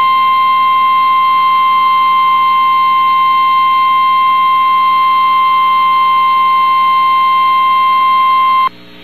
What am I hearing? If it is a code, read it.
Emergency Alert System (SAME) tones (the 1050 Hz attention tone). It is heard from 1.2 to 8.2 s.
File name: eas received alerts / L1 19566 audio